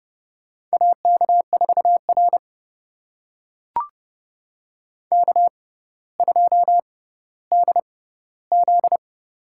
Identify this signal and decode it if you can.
Morse code, heard as a keyed tone, about 32 wpm, spaced as for 16 wpm, 690 Hz. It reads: AK4R K2DZ
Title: USA Call Signs - 4 Characters - 6x Character Spacing - 30wpm